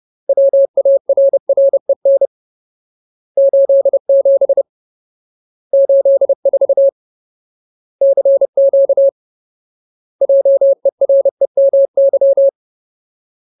Morse code, heard as a keyed tone, about 30 wpm, 550 Hz. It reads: WARREN 87 84 CQ JEREMY
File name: Ultimate Word List / 30WPM Ultimate